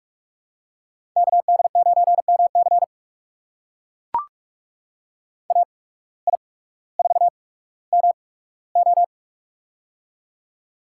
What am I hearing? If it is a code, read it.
Morse code, heard as a keyed tone, about 44 wpm, spaced as for 19 wpm, 700 Hz. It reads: KD9MC AI4MO
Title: USA Call Signs - 5 Characters - 8x Character Spacing - 45wpm